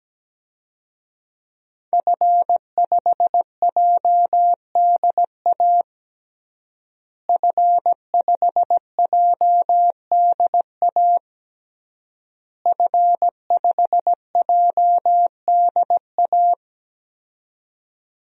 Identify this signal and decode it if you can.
Morse code, heard as a keyed tone, about 17 wpm, 700 Hz. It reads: F5JDA F5JDA F5JDA
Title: DXCC-100 Call Signs - 5 Characters - Repeated 3x - 17wpm